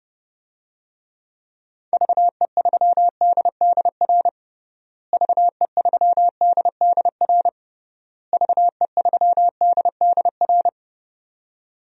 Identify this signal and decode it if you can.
Morse code, heard as a keyed tone, about 32 wpm, 700 Hz. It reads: VE3DDR VE3DDR VE3DDR
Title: DXCC-100 Call Signs - 6 Characters - Repeated 3x - 30wpm